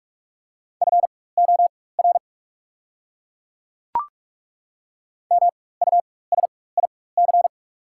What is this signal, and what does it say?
Morse code, heard as a keyed tone, about 43 wpm, spaced as for 21 wpm, 690 Hz. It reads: FOR MUSIC
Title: 4x Character Spacing - Top 300 Words - 45wpm